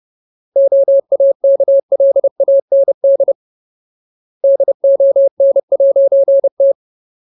Morse code, heard as a keyed tone, about 30 wpm, 550 Hz. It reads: OAKLAND DON'T
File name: Ultimate Word List / 30WPM Ultimate